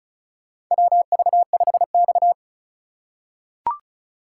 Morse code, heard as a keyed tone, about 36 wpm, 700 Hz. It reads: WV5X
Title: USA Call Signs - 4 Characters - 3x Character Spacing - 35wpm